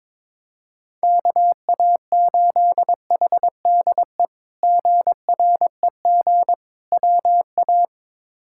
Morse code, heard as a keyed tone, about 22 wpm, 700 Hz. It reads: KA8HDE GREG WA
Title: K1USN SST QSO Practice - 22wpm